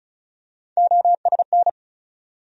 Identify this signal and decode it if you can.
Morse code, heard as a keyed tone, about 34 wpm, 700 Hz. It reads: OSN
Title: USA Call Sign Postfixes - 3 Letters 35wpm